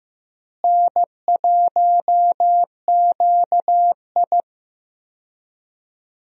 Morse code, heard as a keyed tone, about 15 wpm, 700 Hz. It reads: N1QI
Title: USA Call Signs - 4 Characters - Repeated 3x - 15wpm